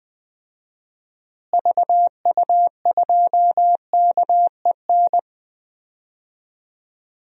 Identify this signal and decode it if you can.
Morse code, heard as a keyed tone, about 20 wpm, 700 Hz. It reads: VU2KEN
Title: DXCC-100 Call Signs - 6 Characters - Repeated 3x - 20wpm